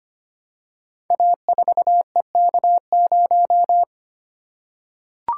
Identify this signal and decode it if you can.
Morse code, heard as a keyed tone, about 25 wpm, 700 Hz. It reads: A4EK0
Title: Code Groups - 5 Letter:Numbers - 25wpm